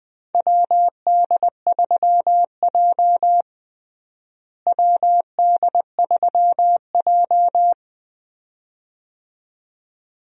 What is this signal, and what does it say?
Morse code, heard as a keyed tone, about 20 wpm, 700 Hz. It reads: WD3J WD3J
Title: USA Call Signs - 4 Characters - Repeated 2x - 20wpm